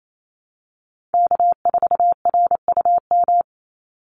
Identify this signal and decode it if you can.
Morse code, heard as a keyed tone, about 28 wpm, 700 Hz. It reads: K4RUM
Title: Call Signs Repeated 3x - 28wpm